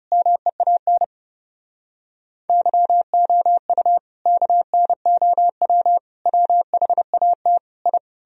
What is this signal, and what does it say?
Morse code, heard as a keyed tone, about 40 wpm, 700 Hz. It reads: MEAN YOU KNOW WHAT I